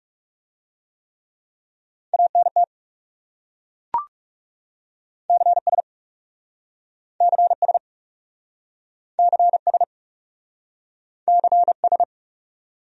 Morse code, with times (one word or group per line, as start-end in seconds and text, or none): 2.1-2.6 ANT
5.3-5.8 CS
7.2-7.8 CS
9.2-9.8 CS
11.3-12.0 CS